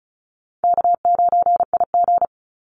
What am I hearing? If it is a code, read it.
Morse code, heard as a keyed tone, about 34 wpm, 700 Hz. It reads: K9IG
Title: Easy Call Signs Repeated 3x - 35wpm